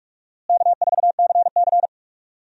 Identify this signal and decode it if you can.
Morse code, heard as a keyed tone, about 46 wpm, 700 Hz. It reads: K4CC
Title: USA Call Signs - 4 Characters - Repeated 3x - 45wpm